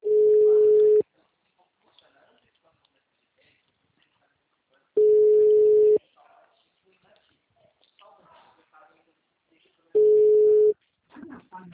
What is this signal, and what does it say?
Ringback tone (Europe, 425 Hz), with speech.
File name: out- -2001- - - .41834